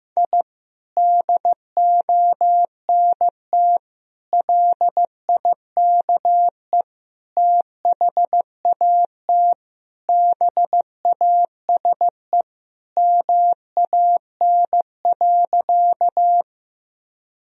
Morse code, heard as a keyed tone, about 16 wpm, 700 Hz. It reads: I DONT LIKE THAT BASE MAN.
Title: Sentences from Top 1000 Words - Review Part 1 - 15wpm